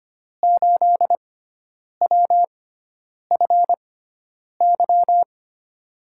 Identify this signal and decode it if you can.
Morse code, heard as a keyed tone, about 25 wpm, spaced as for 9 wpm, 700 Hz. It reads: 8WFY
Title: USA Call Signs - 5 Characters - 6x Character Spacing - 25wpm